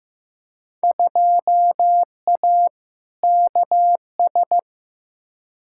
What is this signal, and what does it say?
Morse code, heard as a keyed tone, about 15 wpm, 700 Hz. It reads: 2A KS
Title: Field Day Exchanges - 15wpm